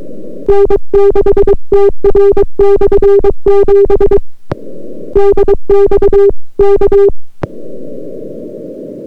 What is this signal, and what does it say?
Morse code, heard as a keyed tone, about 22 wpm, 400 Hz. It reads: N6TR/7 DX K